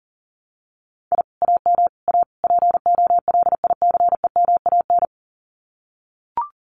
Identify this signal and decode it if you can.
Morse code, heard as a keyed tone, about 38 wpm, 700 Hz. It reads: I AM A POLICEMAN
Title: Common QSO Phrases Encore 40wpm